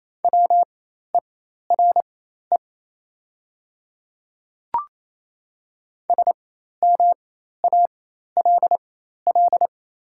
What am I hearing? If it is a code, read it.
Morse code, heard as a keyed tone, about 28 wpm, spaced as for 13 wpm, 690 Hz. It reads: WERE SMALL
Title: 4x Character Spacing - Top 200 Words - 28wpm